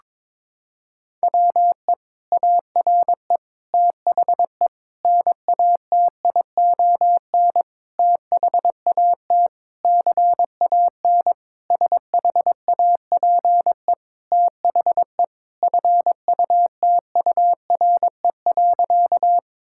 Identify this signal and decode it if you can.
Morse code, heard as a keyed tone, about 22 wpm, 700 Hz. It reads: WE ARE THE NATION THAT CAN SHAPE THE FUTURE.
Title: Sentences from Top 1000 Words - Review Part 6 - 22wpm